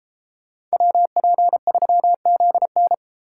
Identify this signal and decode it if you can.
Morse code, heard as a keyed tone, about 31 wpm, 700 Hz. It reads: WP3ZN